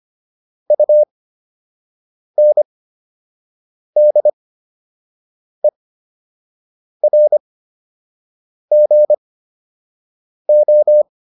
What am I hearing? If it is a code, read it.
Morse code, heard as a keyed tone, about 25 wpm, spaced as for 6 wpm, 600 Hz. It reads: UNDERGO
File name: UN7 06